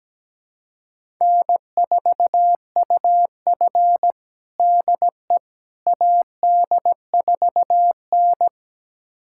Morse code, heard as a keyed tone, about 17 wpm, 700 Hz. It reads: N4UF DE AD4N
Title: USA Callsign DE Callsign - 4 Characters - 17wpm